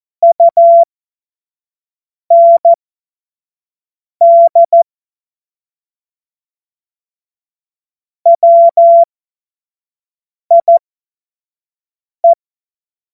Morse code, heard as a keyed tone, about 14 wpm, spaced as for 5 wpm, 680 Hz. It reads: UND WIE
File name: Morsekurs / Witze / Witz 03 Juni 2025 25 F70